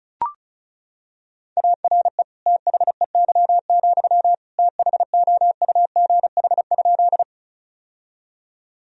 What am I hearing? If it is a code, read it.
Morse code, heard as a keyed tone, about 36 wpm, 700 Hz. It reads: ARE THEY, THOUGH?
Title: Sentences from Top 1000 Words - Review Part 2 - 35wpm